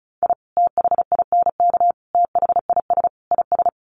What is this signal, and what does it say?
Morse code, heard as a keyed tone, about 34 wpm, 700 Hz. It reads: I THINK THIS IS
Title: Sets of 4 Words 35wpm